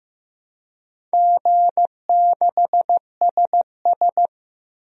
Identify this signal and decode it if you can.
Morse code, heard as a keyed tone, about 16 wpm, 700 Hz. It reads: G6SS
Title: DXCC-100 Call Signs - 4 Characters - Repeated 3x - 15wpm